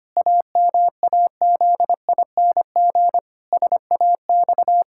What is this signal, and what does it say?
Morse code, heard as a keyed tone, about 25 wpm, 700 Hz. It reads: AMAZING SAX
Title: Pangram Challenge v2 - 25wpm